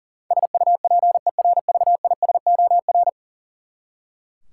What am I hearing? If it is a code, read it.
Morse code, heard as a keyed tone, about 40 wpm, 700 Hz. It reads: SUPERVISOR
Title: Practice QSO - Exchange 3 - More Info - 40wpm